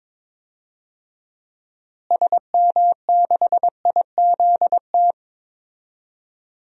Morse code, heard as a keyed tone, about 22 wpm, 700 Hz. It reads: SM6IZT
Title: DXCC-100 Call Signs - 6 Characters - Repeated 3x - 22wpm